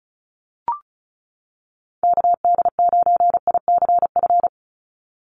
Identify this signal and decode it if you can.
Morse code, heard as a keyed tone, about 34 wpm, 700 Hz. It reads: KD9ICF
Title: USA Call Signs 35wpm